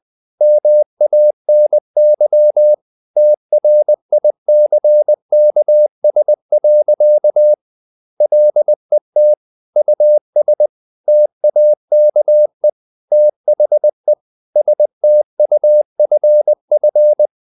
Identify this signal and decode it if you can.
Morse code, heard as a keyed tone, about 20 wpm, 600 Hz. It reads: MANY TRICKS. LET US TAKE THE STUFF
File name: the old man and the sea / the old man and the sea part 1 20wpm